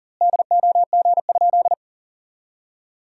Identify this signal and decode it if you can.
Morse code, heard as a keyed tone, about 40 wpm, 700 Hz. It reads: DOG?
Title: 2x Word-Spacing - Sentences from Top 400 Words - 40wpm